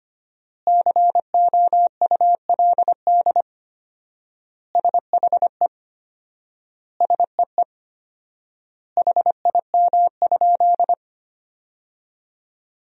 Morse code, heard as a keyed tone, about 25 wpm, 700 Hz. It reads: COULD SHE SEE HIM?
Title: 4x Word-Spacing - Sentences from Top 100 Words - 25wpm